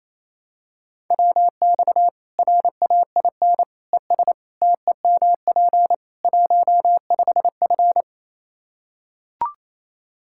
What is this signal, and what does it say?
Morse code, heard as a keyed tone, about 27 wpm, 700 Hz. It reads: WX RAIN ES TEMP 15F
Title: Practice QSO - Exchange 2 - Info - 28wpm